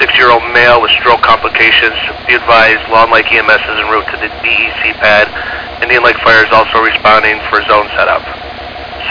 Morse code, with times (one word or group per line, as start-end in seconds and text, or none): none